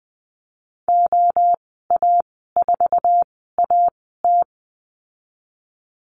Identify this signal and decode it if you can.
Morse code, heard as a keyed tone, about 20 wpm, spaced as for 15 wpm, 700 Hz. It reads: OA4AT
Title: 173 Lesson 28g - Number 9 - Call Signs 20wpm